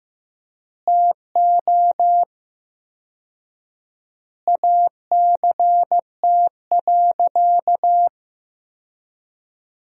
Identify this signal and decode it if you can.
Morse code, heard as a keyed tone, about 15 wpm, 700 Hz. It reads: TO ACT.
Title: 4x Word-Spacing - Sentences from Top 200 Words - 15wpm